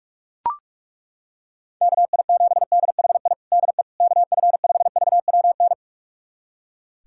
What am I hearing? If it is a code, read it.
Morse code, heard as a keyed tone, about 41 wpm, 690 Hz. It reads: KI7DHI DE KF5VWN
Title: USA Callsign DE Callsign - 6 Characters - 45wpm